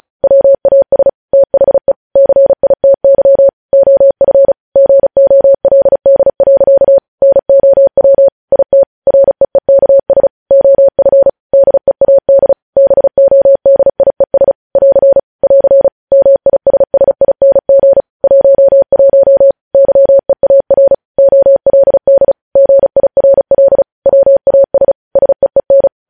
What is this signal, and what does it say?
Morse code, heard as a keyed tone, about 36 wpm, 550 Hz. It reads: WAS THE CITY OF GOLD. NOW IT REEKS OF DEAD BODIES <AR> <AR> MISSING 11 YEAR OLD GIRL WAS SEEN
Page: news headlines in morse code at 35 wpm